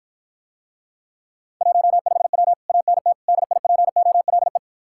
Morse code, heard as a keyed tone, about 54 wpm, 700 Hz. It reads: 15W ANT DIPOLE